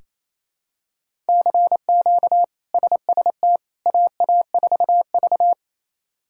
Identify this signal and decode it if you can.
Morse code, heard as a keyed tone, about 28 wpm, 700 Hz. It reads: CQ SST AA4V
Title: K1USN SST QSO Practice - 28wpm